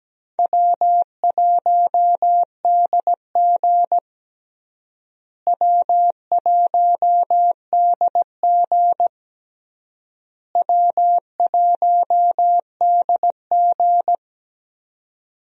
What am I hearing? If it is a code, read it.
Morse code, heard as a keyed tone, about 17 wpm, 700 Hz. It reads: W1DG W1DG W1DG
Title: USA Call Signs - 4 Characters - Repeated 3x - 17wpm